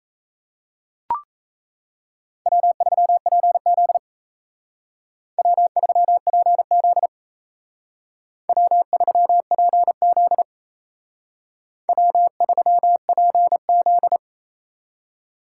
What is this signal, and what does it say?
Morse code, heard as a keyed tone, about 36 wpm, 700 Hz. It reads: W3PZ W3PZ W3PZ W3PZ